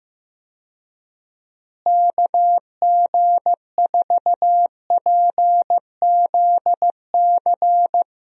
Morse code, heard as a keyed tone, about 15 wpm, 700 Hz. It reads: KG4PZC
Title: USA Call Signs - 6 Characters - Repeated 3x - 15wpm